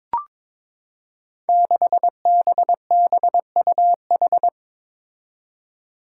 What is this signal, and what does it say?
Morse code, heard as a keyed tone, about 22 wpm, 700 Hz. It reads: 6BBUH
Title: Five Challenging Letter-Number Combinations 22wpm